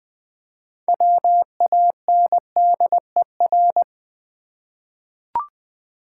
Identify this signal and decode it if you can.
Morse code, heard as a keyed tone, about 20 wpm, 700 Hz. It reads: WANDER
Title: 6 Letter Words - First 3 Letters Given - 20wpm